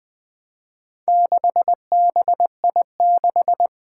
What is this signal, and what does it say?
Morse code, heard as a keyed tone, about 20 wpm, 700 Hz. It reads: 6BI6
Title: Four Challenging Letter-Number Combinations 20wpm